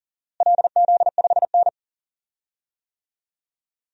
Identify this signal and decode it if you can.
Morse code, heard as a keyed tone, about 40 wpm, 700 Hz. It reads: LZ5N
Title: DXCC-100 Call Signs - 4 Characters - Repeated 2x - 40wpm